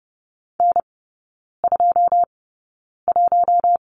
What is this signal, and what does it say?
Morse code, heard as a keyed tone, about 30 wpm, spaced as for 9 wpm, 700 Hz. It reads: N21
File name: Lesson 16e - Letter c - Three Characters 30wpm